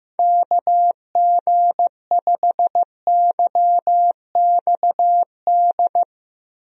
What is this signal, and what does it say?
Morse code, heard as a keyed tone, about 16 wpm, 700 Hz. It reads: KG5YXD
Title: USA Call Signs - 6 Characters - Repeated 3x - 15wpm